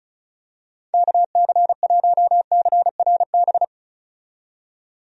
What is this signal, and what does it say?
Morse code, heard as a keyed tone, about 36 wpm, 700 Hz. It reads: KC1CRB
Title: USA Call Signs - 6 Characters - 35wpm